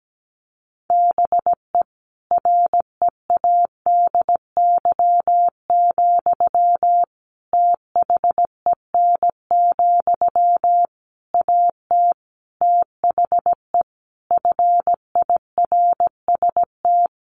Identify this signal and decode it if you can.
Morse code, heard as a keyed tone, about 17 wpm, 700 Hz. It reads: BE READY, THEN, AT THE FIRST
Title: Sentences from Top 1400 Words 17wpm